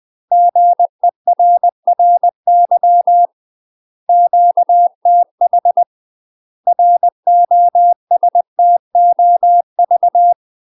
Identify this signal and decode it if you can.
Morse code, heard as a keyed tone, about 20 wpm, 700 Hz. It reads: GERRY QTH ROSTOV
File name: UK MASTER QSOs TRAINING / 20WPM / UK MASTER QSOs 0014